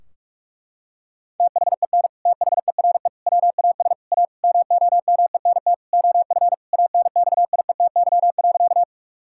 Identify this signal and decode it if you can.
Morse code, heard as a keyed tone, about 46 wpm, 700 Hz. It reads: THEN THERE WAS A MOMENT OF ANXIETY.
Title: Sentences from Top 1600 Words 45wpm